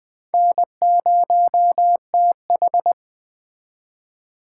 Morse code, heard as a keyed tone, about 22 wpm, 700 Hz. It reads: N0TH